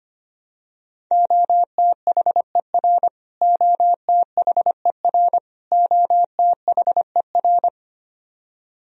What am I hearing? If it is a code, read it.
Morse code, heard as a keyed tone, about 25 wpm, 700 Hz. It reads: OTHER OTHER OTHER
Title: Top 100 Words Repeated 25wpm